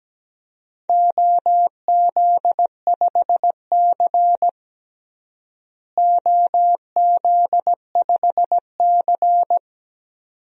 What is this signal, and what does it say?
Morse code, heard as a keyed tone, about 17 wpm, 700 Hz. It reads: OZ5C OZ5C
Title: DXCC-100 Call Signs - 4 Characters - Repeated 2x - 17wpm